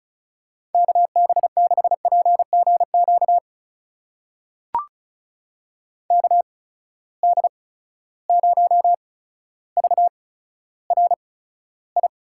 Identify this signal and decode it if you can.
Morse code, heard as a keyed tone, about 36 wpm, spaced as for 15 wpm, 700 Hz. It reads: KB6PGQ KD0VRI
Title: USA Call Signs - 6 Characters - 8x Character Spacing - 35wpm